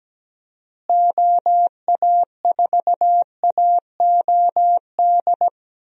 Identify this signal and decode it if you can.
Morse code, heard as a keyed tone, about 17 wpm, 700 Hz. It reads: OA4AOD
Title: USA Call Signs - 6 Characters - Repeated 2x - 17wpm